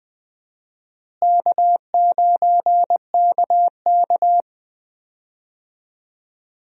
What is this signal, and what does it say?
Morse code, heard as a keyed tone, about 20 wpm, 700 Hz. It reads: K9KK
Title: USA Call Signs - 4 Characters - Repeated 3x - 20wpm